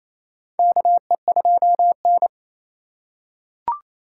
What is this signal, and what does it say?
Morse code, heard as a keyed tone, about 28 wpm, 700 Hz. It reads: KE2N
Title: USA Call Signs - 4 Characters - 2x Character Spacing - 28wpm